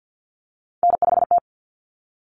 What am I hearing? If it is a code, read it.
Morse code, heard as a keyed tone, about 50 wpm, 700 Hz. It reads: N5T
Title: Common Cut Numbers - 3-Digit 50wpm